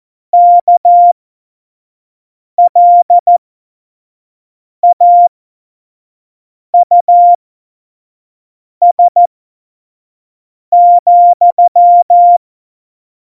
Morse code, heard as a keyed tone, about 14 wpm, spaced as for 5 wpm, 700 Hz. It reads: KLAUS,
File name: Morsekurs / Witze / Witz 16. April 2019, 25 F70